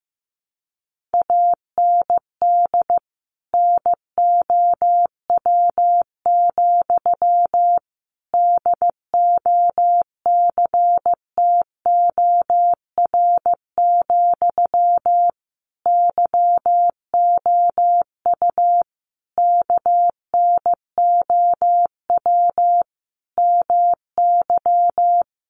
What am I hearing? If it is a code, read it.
Morse code, heard as a keyed tone, about 15 wpm, 700 Hz. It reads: AND NOW, DOCTOR, YOU KNOW MY
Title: Sentences from Top 1500 Words 15wpm